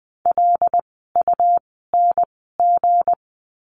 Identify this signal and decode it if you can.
Morse code, heard as a keyed tone, about 20 wpm, spaced as for 15 wpm, 700 Hz. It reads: LUNG
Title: 160 Lesson 26f - Letter g - Words 20wpm